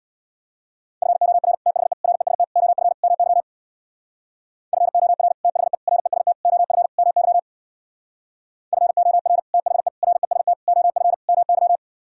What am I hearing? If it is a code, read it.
Morse code, heard as a keyed tone, about 64 wpm, 700 Hz. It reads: FOR THE REST OF MY FOR THE REST OF MY FOR THE REST OF MY